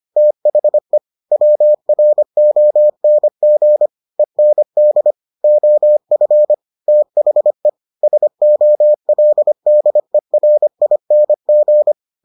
Morse code, heard as a keyed tone, about 25 wpm, 600 Hz. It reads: THE WRONG END OF THE SOLDERING